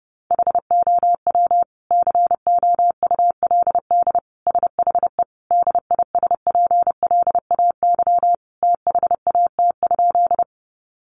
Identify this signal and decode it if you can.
Morse code, heard as a keyed tone, about 30 wpm, 700 Hz. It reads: HOW COULD SHE DISPLAY THAT?
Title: Sentences from Top 1200 Words 30wpm